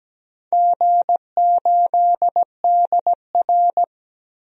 Morse code, heard as a keyed tone, about 17 wpm, 700 Hz. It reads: G8DR
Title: DXCC-100 Call Signs - 4 Characters - 17wpm